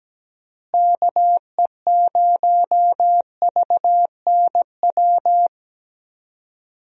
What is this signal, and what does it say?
Morse code, heard as a keyed tone, about 17 wpm, 700 Hz. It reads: KE0VNW
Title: USA Call Signs - 6 Characters - Repeated 3x - 17wpm